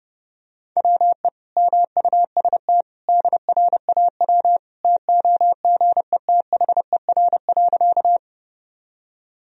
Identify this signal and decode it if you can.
Morse code, heard as a keyed tone, about 30 wpm, 700 Hz. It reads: WE MUST DRAW TOGETHER.
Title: Sentences from Top 1000 Words - Review Part 1 - 30wpm